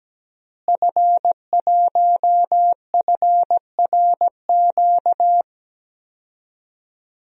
Morse code, heard as a keyed tone, about 17 wpm, 700 Hz. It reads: F1FRQ
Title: DXCC-100 Call Signs - 5 Characters - Repeated 3x - 17wpm